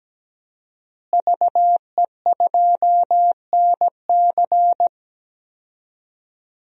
Morse code, heard as a keyed tone, about 17 wpm, 700 Hz. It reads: VE2NC
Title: DXCC-100 Call Signs - 5 Characters - Repeated 2x - 17wpm